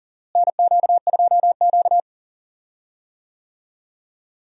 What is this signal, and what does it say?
Morse code, heard as a keyed tone, about 40 wpm, 700 Hz. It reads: NQ2Q
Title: USA Call Signs - 4 Characters - Repeated 3x - 40wpm